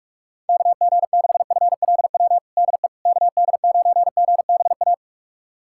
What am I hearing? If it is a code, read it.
Morse code, heard as a keyed tone, about 45 wpm, 700 Hz. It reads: KG6FLW DE KD9GBA
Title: USA Callsign DE Callsign 45wpm